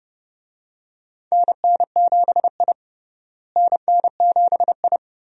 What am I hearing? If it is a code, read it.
Morse code, heard as a keyed tone, about 30 wpm, 700 Hz. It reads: NN7I NN7I
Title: USA Call Signs - 4 Characters - Repeated 2x - 30wpm